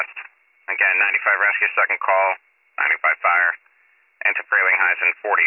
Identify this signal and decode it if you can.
Emergency Alert System (SAME) tones, heard as harsh buzzing bursts, with speech. It is heard from 1.1 to 5.0 s.